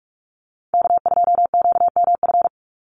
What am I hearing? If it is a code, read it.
Morse code, heard as a keyed tone, about 44 wpm, 700 Hz. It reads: K2QMF
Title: Call Signs Repeated 2x - 45wpm